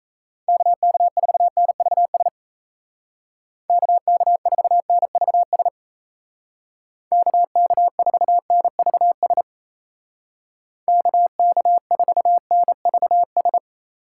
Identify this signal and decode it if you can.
Morse code, heard as a keyed tone, about 35 wpm, 700 Hz. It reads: KK4NVS KK4NVS KK4NVS KK4NVS
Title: Speed Racing - USA Call Signs 28wpm